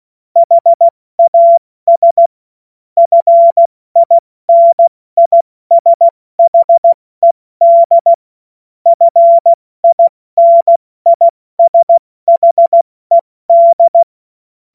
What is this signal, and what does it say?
Morse code, heard as a keyed tone, about 16 wpm, 680 Hz. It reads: HAS FINISHED FINISHED
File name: Morsekurs / Witze / Witz 27. September 2022 70 F80